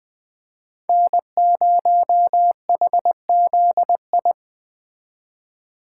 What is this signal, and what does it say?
Morse code, heard as a keyed tone, about 21 wpm, 700 Hz. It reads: N0HZI